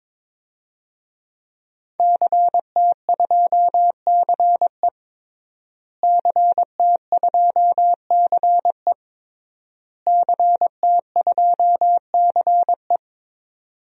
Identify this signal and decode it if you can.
Morse code, heard as a keyed tone, about 22 wpm, 700 Hz. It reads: CT2CE CT2CE CT2CE
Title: DXCC-100 Call Signs - 5 Characters - Repeated 3x - 22wpm